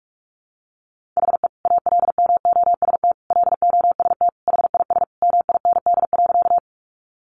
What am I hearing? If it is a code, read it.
Morse code, heard as a keyed tone, about 44 wpm, 700 Hz. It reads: HE ALMOST LOST HIS MIND.
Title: Sentences from Top 1400 Words 45wpm